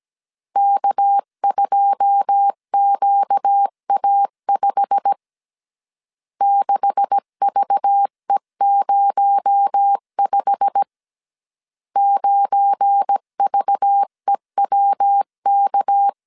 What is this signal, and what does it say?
Morse code, heard as a keyed tone, about 17 wpm, 800 Hz. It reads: K2QA5 6VE05 9VEWK
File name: 1500C S B 16wpm